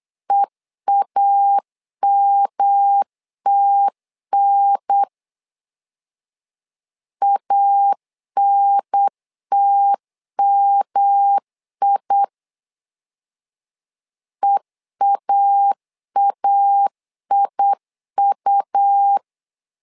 Morse code, heard as a keyed tone, about 8 wpm, 800 Hz. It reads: EAMTN ANTMI EAAIU
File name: LEZIONI / 3 MWRO / 3 M 8wpm 2300